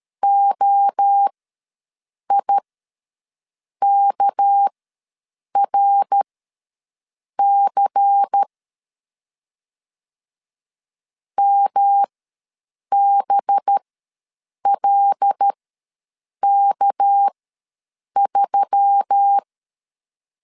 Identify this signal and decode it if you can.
Morse code, heard as a keyed tone, about 13 wpm, spaced as for 7 wpm, 800 Hz. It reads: OIKRC MBLK3
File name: LEZIONI / 6 PKX0 / 6 PK ws 12wpm 3116